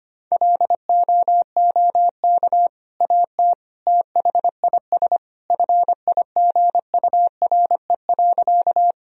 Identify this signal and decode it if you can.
Morse code, heard as a keyed tone, about 25 wpm, 700 Hz. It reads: LOOK AT THIS FIGURE.
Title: Sentences from Top 1000 Words - Review Part 2 - 25wpm